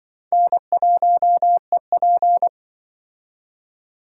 Morse code, heard as a keyed tone, about 24 wpm, 700 Hz. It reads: N1EP
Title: Speed Racing - USA 1x2 Call Signs 20wpm